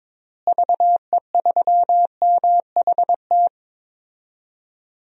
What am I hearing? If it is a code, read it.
Morse code, heard as a keyed tone, about 22 wpm, 700 Hz. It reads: VE3MHT
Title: DXCC-100 Call Signs - 6 Characters - Repeated 3x - 22wpm